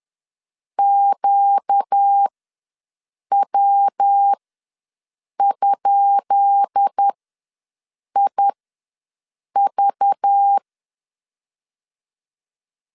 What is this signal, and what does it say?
Morse code, heard as a keyed tone, about 10 wpm, spaced as for 6 wpm, 800 Hz. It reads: QW?IV
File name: LEZIONI / 10 SIGNS / 10B INT ws 10wpm 4134